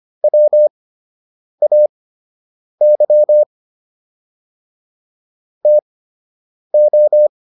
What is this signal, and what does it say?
Morse code, heard as a keyed tone, about 25 wpm, spaced as for 8 wpm, 600 Hz. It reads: WAY TO